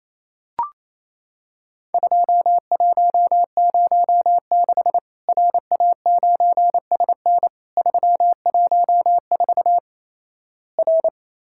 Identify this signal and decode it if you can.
Morse code, heard as a keyed tone, about 28 wpm, 700 Hz. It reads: 2106 RA9SN 314 R
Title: Worked All Europe DX Contest - QTC Exchange - 28wpm